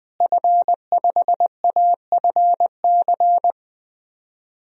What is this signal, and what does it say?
Morse code, heard as a keyed tone, about 20 wpm, 700 Hz. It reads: F5AFC
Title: DXCC-100 Call Signs - 5 Characters - Repeated 3x - 20wpm